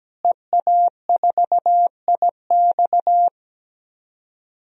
Morse code, heard as a keyed tone, about 17 wpm, 700 Hz. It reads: EA4IX
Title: DXCC-100 Call Signs - 5 Characters - Repeated 3x - 17wpm